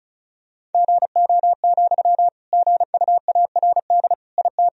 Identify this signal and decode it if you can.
Morse code, heard as a keyed tone, about 36 wpm, 700 Hz. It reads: GO, GUARD IT
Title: Sentences from Top 1000 Words - Review Part 4 - 35wpm